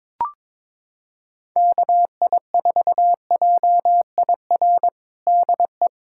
Morse code, heard as a keyed tone, about 22 wpm, 700 Hz. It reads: KI4JIR DE
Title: USA Callsign DE Callsign - 6 Characters - 22wpm